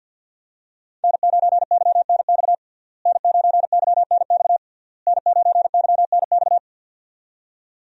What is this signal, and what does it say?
Morse code, heard as a keyed tone, about 53 wpm, 690 Hz. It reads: N9YNX N9YNX N9YNX
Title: USA Call Signs - 5 Characters - Repeated 3x - 50wpm